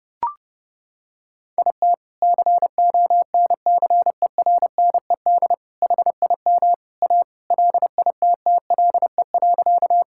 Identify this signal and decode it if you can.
Morse code, heard as a keyed tone, about 31 wpm, 700 Hz. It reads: IT CONCERNED HIM A LITTLE.
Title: Sentences from Top 1000 Words - Review Part 6 - 30wpm